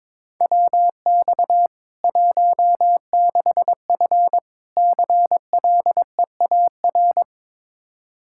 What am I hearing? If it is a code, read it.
Morse code, heard as a keyed tone, about 22 wpm, 700 Hz. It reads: WX 16F CLEAR
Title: Practice QSO - Exchange 2 - Info - 22wpm